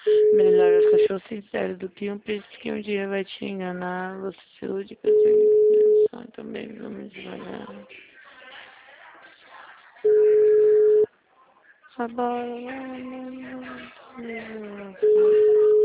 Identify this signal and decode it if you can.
Ringback tone (Europe, 425 Hz), with speech.